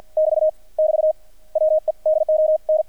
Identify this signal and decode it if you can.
Morse code, heard as a keyed tone, about 59 wpm, spaced as for 50 wpm, 610 Hz. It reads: <BT> <BT> WE DON
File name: archived / CRSnet